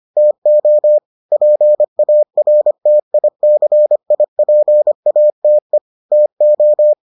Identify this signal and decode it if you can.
Morse code, heard as a keyed tone, about 25 wpm, 600 Hz. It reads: TO PARTICIPATE TO